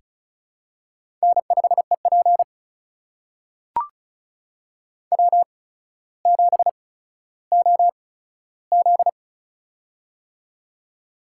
Morse code, heard as a keyed tone, about 36 wpm, spaced as for 15 wpm, 700 Hz. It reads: N5EP W7OZ
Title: USA Call Signs - 4 Characters - 8x Character Spacing - 35wpm